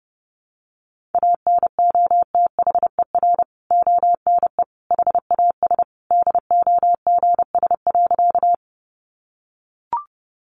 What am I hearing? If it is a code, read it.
Morse code, heard as a keyed tone, about 30 wpm, 700 Hz. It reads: ANOTHER ONE HAS DOGS.
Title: Additional Easy Phrases 30wpm